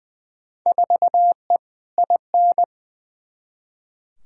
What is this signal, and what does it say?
Morse code, heard as a keyed tone, about 20 wpm, 700 Hz. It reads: 4E IN
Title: Field Day Exchanges - 20wpm